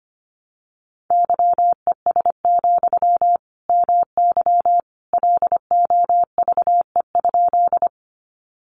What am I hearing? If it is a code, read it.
Morse code, heard as a keyed tone, about 25 wpm, 700 Hz. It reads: YES, MY LOVE?
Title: Sentences from Top 500 Words 25wpm